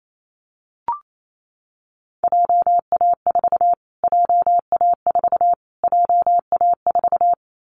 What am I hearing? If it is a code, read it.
Morse code, heard as a keyed tone, about 28 wpm, 700 Hz. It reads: JA4 JA4 JA4
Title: Callsign Prefixes for Top 25 DXCC - Repeated 3x - 28wpm